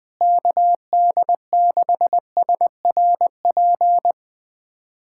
Morse code, heard as a keyed tone, about 20 wpm, 700 Hz. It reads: KD6SRP
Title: USA Callsign DE Callsign 20wpm